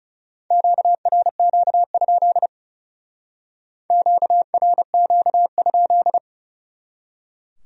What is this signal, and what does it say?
Morse code, heard as a keyed tone, about 37 wpm, 700 Hz. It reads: QRQ? QRQ?
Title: Speed-Racing Q-Codes - 30wpm